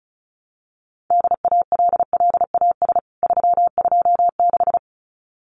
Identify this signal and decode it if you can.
Morse code, heard as a keyed tone, about 34 wpm, 700 Hz. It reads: DALLAS 326
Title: CWT Exchange 35wpm